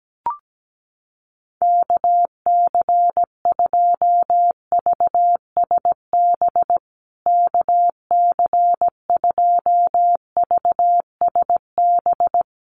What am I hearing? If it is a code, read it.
Morse code, heard as a keyed tone, about 17 wpm, 700 Hz. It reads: KC2VSB KC2VSB
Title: USA Call Signs Repeated 3x 17wpm